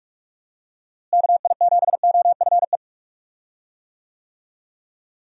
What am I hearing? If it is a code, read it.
Morse code, heard as a keyed tone, about 43 wpm, 700 Hz. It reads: KI7OFE
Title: USA Call Signs - 6 Characters - Repeated 3x - 45wpm